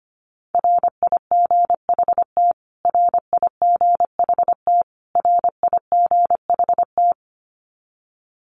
Morse code, heard as a keyed tone, about 25 wpm, 700 Hz. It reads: RIGHT RIGHT RIGHT
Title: Top 200 Words Repeated 25wpm